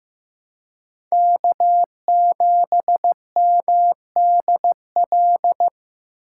Morse code, heard as a keyed tone, about 15 wpm, 700 Hz. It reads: K7MDL
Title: USA Call Signs - 8x Character Spacing - 15wpm